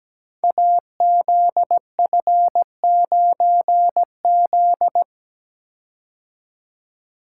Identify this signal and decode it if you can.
Morse code, heard as a keyed tone, about 17 wpm, 700 Hz. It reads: AZF9Z
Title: Code Groups - 5 Letter:Numbers - 17wpm